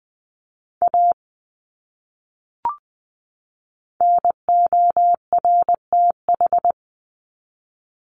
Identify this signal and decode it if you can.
Morse code, heard as a keyed tone, about 20 wpm, 700 Hz. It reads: A NORTH
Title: Top 300 Words 20wpm